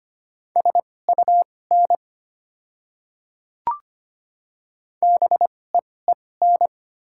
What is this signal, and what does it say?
Morse code, heard as a keyed tone, about 25 wpm, spaced as for 18 wpm, 700 Hz. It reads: SUN BEEN
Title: 2x Character Spacing - Top 300 Words - 25wpm